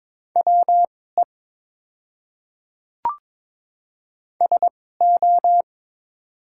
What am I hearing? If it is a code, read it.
Morse code, heard as a keyed tone, about 22 wpm, spaced as for 16 wpm, 700 Hz. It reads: WE SO
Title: 2x Character Spacing - Top 200 Words - 22wpm